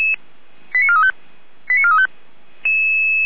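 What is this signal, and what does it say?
Five-tone selective call (ZVEI), heard as quick runs of five tones, tones of about 70 ms. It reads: 89435 89435
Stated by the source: ZVEI five-tone selective call received on an SDR.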